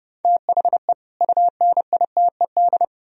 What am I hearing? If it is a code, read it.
Morse code, heard as a keyed tone, about 40 wpm, 700 Hz. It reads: THE UNITED